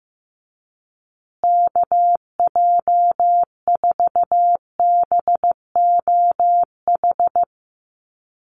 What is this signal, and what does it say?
Morse code, heard as a keyed tone, about 15 wpm, 700 Hz. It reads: KJ4BOH
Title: USA Call Signs Repeated 2x 15wpm